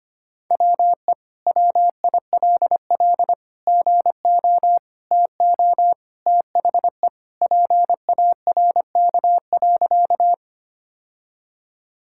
Morse code, heard as a keyed tone, about 25 wpm, 700 Hz. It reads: WE WILL GO TO THE PARK.
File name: practice / 25wpm / Sentences with 2-Word Phrases - 25wpm